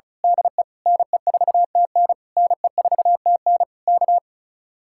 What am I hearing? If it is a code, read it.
Morse code, heard as a keyed tone, about 32 wpm, 700 Hz. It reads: DE NE4TN NE4TN K
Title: Speed-Racing Common QSO Phrases - 35wpm